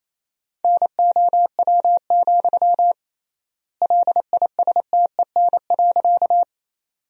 Morse code, heard as a keyed tone, about 28 wpm, 700 Hz. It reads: NOW, LISTEN.
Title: 3x Word-Spacing - Sentences from Top 400 Words - 28wpm